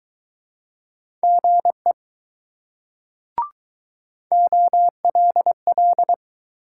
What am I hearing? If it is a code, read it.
Morse code, heard as a keyed tone, about 20 wpm, 700 Hz. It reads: GE OLL